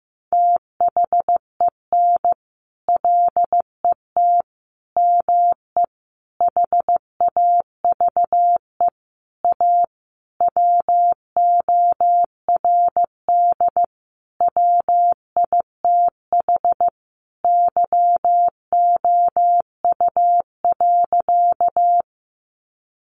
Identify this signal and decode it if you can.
Morse code, heard as a keyed tone, about 15 wpm, 700 Hz. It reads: THEN LET ME HAVE A WORD WITH YOU.
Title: Sentences from Top 300 Words 15wpm